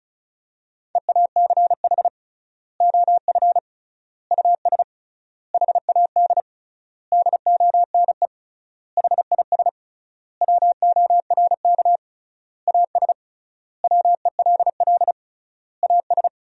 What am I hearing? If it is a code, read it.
Morse code, heard as a keyed tone, about 36 wpm, 700 Hz. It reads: EACH OF US HAD DONE HIS WORK AS WELL AS
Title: 3x Word-Spacing - Sentences from Top 500 Words - 35wpm